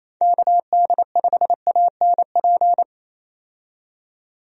Morse code, heard as a keyed tone, about 28 wpm, 700 Hz. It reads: KD5ANP
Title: USA Call Signs - 6 Characters - 28wpm